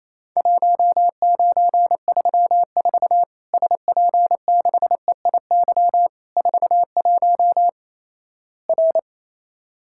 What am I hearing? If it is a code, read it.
Morse code, heard as a keyed tone, about 28 wpm, 700 Hz. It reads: 1934 SP6EIY 41 R
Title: Worked All Europe DX Contest - QTC Exchange - 28wpm